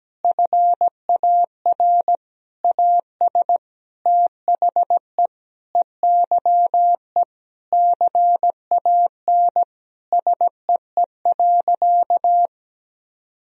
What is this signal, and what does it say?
Morse code, heard as a keyed tone, about 17 wpm, 700 Hz. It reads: FAR AS THE EYE CAN SEE.
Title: Sentences from Top 1000 Words - Review Part 1 - 17wpm